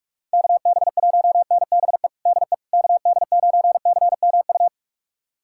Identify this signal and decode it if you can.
Morse code, heard as a keyed tone, about 46 wpm, 700 Hz. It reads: KB1NBE DE KD9CMU
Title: USA Callsign DE Callsign - 6 Characters - 45wpm